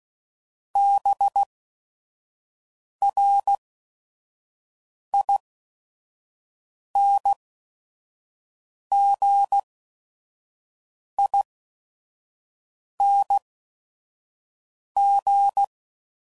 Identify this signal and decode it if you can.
Morse code, heard as a keyed tone, about 16 wpm, spaced as for 5 wpm, 800 Hz. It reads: BRINGING